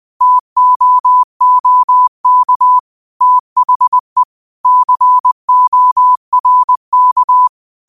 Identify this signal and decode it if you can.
Morse code, heard as a keyed tone, about 20 wpm, 1000 Hz. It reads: TOOK THE CORK
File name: qotdfiles / 20wpm